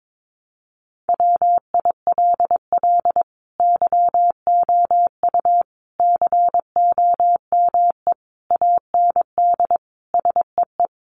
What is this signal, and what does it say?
Morse code, heard as a keyed tone, about 22 wpm, 700 Hz. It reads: WILL YOU COME AND SEE
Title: Sentences from Top 100 Words 22wpm